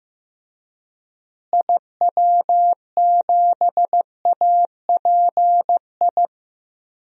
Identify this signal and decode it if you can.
Morse code, heard as a keyed tone, about 15 wpm, 700 Hz. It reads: IW7API
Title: DXCC-100 Call Signs - 6 Characters - Repeated 3x - 15wpm